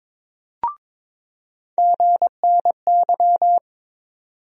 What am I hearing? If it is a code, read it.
Morse code, heard as a keyed tone, about 22 wpm, 700 Hz. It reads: GNY
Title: USA Call Sign Postfixes - 3 Letters 22wpm